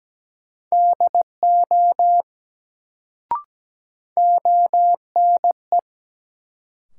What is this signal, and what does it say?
Morse code, heard as a keyed tone, about 17 wpm, 700 Hz. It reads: DO ONE
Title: Sets of 2 Words - New Word Drill - 17wpm